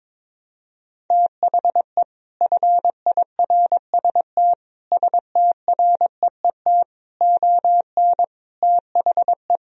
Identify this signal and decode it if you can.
Morse code, heard as a keyed tone, about 22 wpm, 700 Hz. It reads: THE FIRST STREET ON THE
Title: Sentences from Top 1000 Words - Review Part 3 - 22wpm